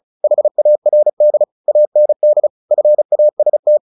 Morse code, heard as a keyed tone, about 34 wpm, 600 Hz. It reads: HARD AND FAST
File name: the old man and the sea / the old man and the sea 1 35wpm